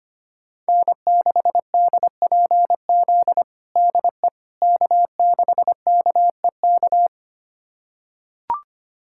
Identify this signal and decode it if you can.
Morse code, heard as a keyed tone, about 25 wpm, 700 Hz. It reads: N6DPZ DE K6KEK
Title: USA Callsign DE Callsign - 5 Characters - 25wpm